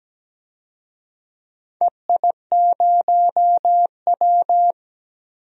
Morse code, heard as a keyed tone, about 17 wpm, 700 Hz. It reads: EI0W
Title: DXCC-100 Call Signs - 4 Characters - 17wpm